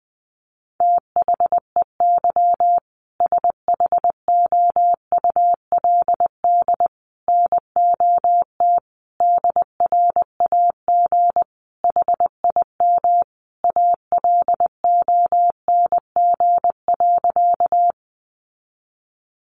Morse code, heard as a keyed tone, about 20 wpm, 700 Hz. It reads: THEY SHOULD NOT DRAG HIM ALONG.
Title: Sentences from Top 1500 Words 20wpm